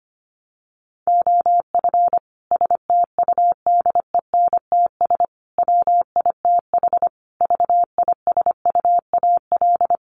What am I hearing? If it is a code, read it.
Morse code, heard as a keyed tone, about 25 wpm, 700 Hz. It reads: OF STUDENTS WITH VISUAL
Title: Sets of 5 Words Encore 25wpm